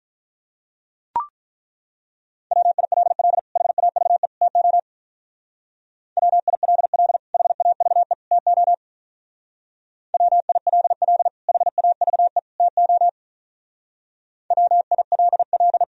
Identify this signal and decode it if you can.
Morse code, heard as a keyed tone, about 43 wpm, 690 Hz. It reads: WILL HAVE TO WILL HAVE TO WILL HAVE TO WILL
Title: Speed-Racing - Sets of 3 Words 35wpm